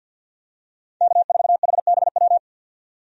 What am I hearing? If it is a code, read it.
Morse code, heard as a keyed tone, about 52 wpm, 700 Hz. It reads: K4HBW
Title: USA Call Signs - 5 Characters - 50wpm